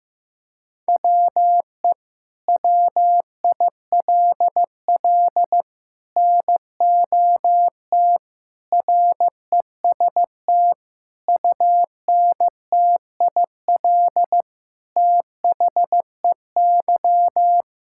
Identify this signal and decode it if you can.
Morse code, heard as a keyed tone, about 15 wpm, 700 Hz. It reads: WE WILL NOT REST UNTIL THEY
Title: Sentences from Top 1600 Words 15wpm